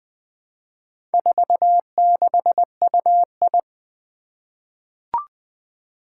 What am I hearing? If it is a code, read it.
Morse code, heard as a keyed tone, about 21 wpm, 700 Hz. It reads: 46UI
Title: Four Challenging Letter-Number Combinations 20wpm